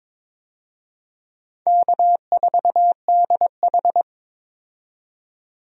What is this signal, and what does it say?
Morse code, heard as a keyed tone, about 21 wpm, 700 Hz. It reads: K4DH
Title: USA Call Signs - 4 Characters - 22wpm